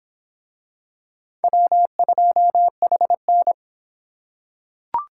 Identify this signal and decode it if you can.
Morse code, heard as a keyed tone, about 24 wpm, 700 Hz. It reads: W2HN